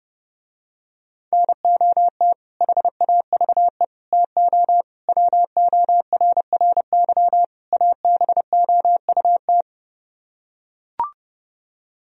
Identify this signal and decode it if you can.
Morse code, heard as a keyed tone, about 30 wpm, 700 Hz. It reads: NOT HAVE TO WORRY ABOUT